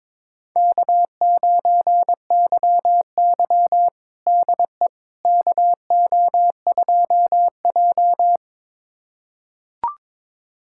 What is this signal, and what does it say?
Morse code, heard as a keyed tone, about 22 wpm, 700 Hz. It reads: K9YY DE KO2J
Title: USA Callsign DE Callsign - 4 Characters - 22wpm